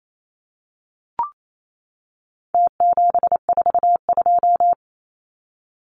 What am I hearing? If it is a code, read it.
Morse code, heard as a keyed tone, about 28 wpm, 700 Hz. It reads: T742
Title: Numbers - Contest Style - 4-Digit 28wpm